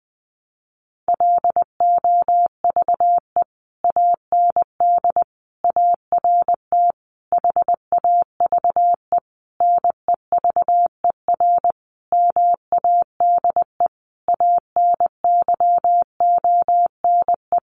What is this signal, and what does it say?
Morse code, heard as a keyed tone, about 20 wpm, 700 Hz. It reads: LOVE AND ART HAVE NEVER MADE ANYONE
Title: Sentences from Top 1100 Words 20wpm